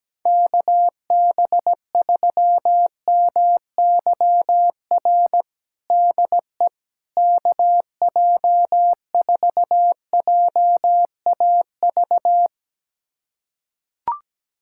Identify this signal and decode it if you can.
Morse code, heard as a keyed tone, about 17 wpm, 700 Hz. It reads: KB3MYR DE KJ4JAV
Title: USA Callsign DE Callsign - 6 Characters - 17wpm